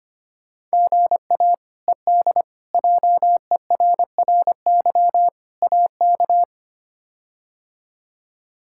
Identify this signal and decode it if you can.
Morse code, heard as a keyed tone, about 25 wpm, 700 Hz. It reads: GA ED JERRY AK
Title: K1USN SST QSO Practice - 25wpm